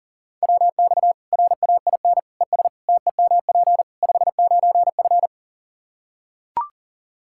Morse code, heard as a keyed tone, about 40 wpm, 700 Hz. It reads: WX RAIN ES TEMP 59F
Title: Practice QSO - Exchange 2 - Info - 40wpm